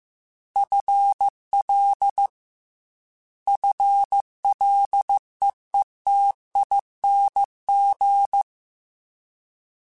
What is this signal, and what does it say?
Morse code, heard as a keyed tone, about 15 wpm, 800 Hz. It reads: FL FLEETING
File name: Combinations / Combinations-mp3s / Combinations@13